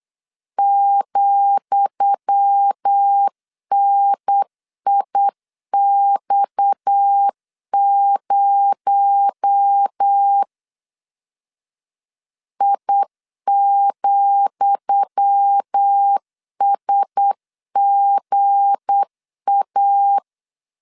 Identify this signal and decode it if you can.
Morse code, heard as a keyed tone, about 8 wpm, 800 Hz. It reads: ,NIX0 I,SGA
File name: LEZIONI / 10 SIGNS / 10D COMA 8wpm 3749